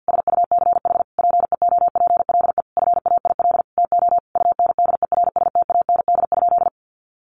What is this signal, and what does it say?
Morse code, heard as a keyed tone, about 50 wpm, 700 Hz. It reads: SUCH PEOPLE FAIL TO UNDERSTAND?
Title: Sentences from Top 1100 Words 50wpm